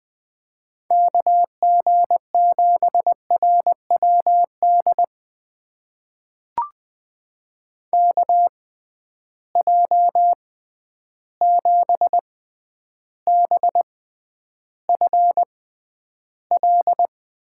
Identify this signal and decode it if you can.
Morse code, heard as a keyed tone, about 20 wpm, spaced as for 10 wpm, 700 Hz. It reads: KG7RWD KJ7BFL
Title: USA Call Signs - 6 Characters - 6x Character Spacing - 20wpm